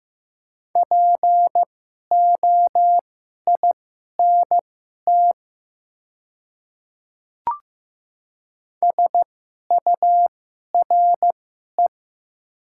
Morse code, heard as a keyed tone, about 15 wpm, spaced as for 11 wpm, 700 Hz. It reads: POINT SURE
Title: 2x Character Spacing - Top 300 Words - 15wpm